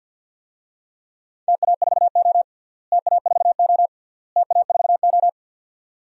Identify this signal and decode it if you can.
Morse code, heard as a keyed tone, about 52 wpm, 700 Hz. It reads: TA4O TA4O TA4O
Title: DXCC-100 Call Signs - 4 Characters - Repeated 3x - 50wpm